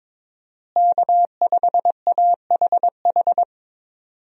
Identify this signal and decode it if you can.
Morse code, heard as a keyed tone, about 21 wpm, 700 Hz. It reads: K5AHH
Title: USA Call Signs - 5 Characters - Repeated 3x - 22wpm